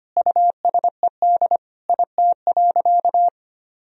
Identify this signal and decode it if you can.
Morse code, heard as a keyed tone, about 25 wpm, 700 Hz. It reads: USED IT.
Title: Sentences from Top 1000 Words - Review Part 2 - 25wpm